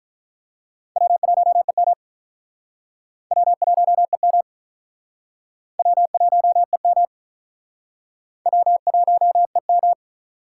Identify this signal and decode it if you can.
Morse code, heard as a keyed tone, about 45 wpm, 700 Hz. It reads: W1EM W1EM W1EM W1EM